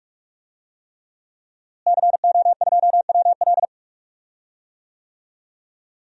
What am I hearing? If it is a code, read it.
Morse code, heard as a keyed tone, about 41 wpm, 700 Hz. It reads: CO2WL
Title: DXCC-100 Call Signs - 5 Characters - Repeated 3x - 45wpm